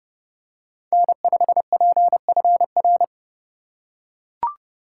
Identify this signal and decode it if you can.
Morse code, heard as a keyed tone, about 32 wpm, 700 Hz. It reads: N5PFR
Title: USA Call Signs - 5 Characters - 2x Character Spacing - 30wpm